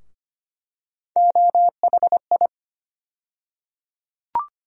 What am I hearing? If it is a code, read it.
Morse code, heard as a keyed tone, about 25 wpm, 700 Hz. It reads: OHI
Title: USA Call Sign Postfixes - 3 Letters 25wpm